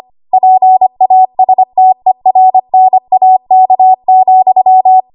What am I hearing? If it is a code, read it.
Morse code, heard as a keyed tone, about 25 wpm, 750 Hz. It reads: PASTERNAK,
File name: Archive / 25 WPM / 25WPM